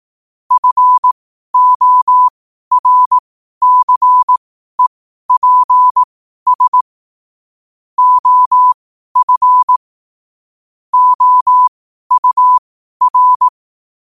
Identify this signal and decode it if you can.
Morse code, heard as a keyed tone, about 18 wpm, spaced as for 12 wpm, 1000 Hz. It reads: FORCEPS OF OUR
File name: qotdfiles / 12wpm